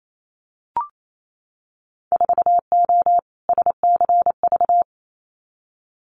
Morse code, heard as a keyed tone, about 28 wpm, 700 Hz. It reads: 4O SCV
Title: Winter Field Day Exchanges 28wpm